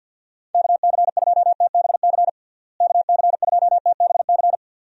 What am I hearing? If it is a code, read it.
Morse code, heard as a keyed tone, about 47 wpm, 700 Hz. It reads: KC2TBC KC2TBC
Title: USA Call Signs - 6 Characters - Repeated 3x - 50wpm